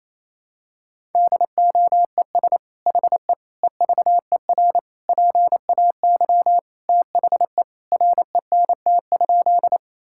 Morse code, heard as a keyed tone, about 28 wpm, 700 Hz. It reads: DOES HE EVER PAY THE RENT?
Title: Sentences from Top 1000 Words - Review Part 5 - 28wpm